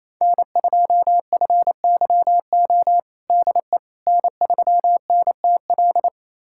Morse code, heard as a keyed tone, about 28 wpm, 700 Hz. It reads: N2FYO DE N3NTL
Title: USA Callsign DE Callsign - 5 Characters - 28wpm